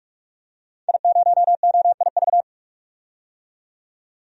Morse code, heard as a keyed tone, about 46 wpm, 700 Hz. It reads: I0OIV
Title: DXCC-100 Call Signs - 5 Characters - 45wpm